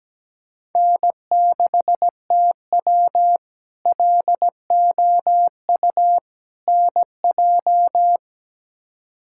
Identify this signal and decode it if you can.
Morse code, heard as a keyed tone, about 17 wpm, 700 Hz. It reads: N6TW LOU NJ
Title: K1USN SST QSO Practice - 17wpm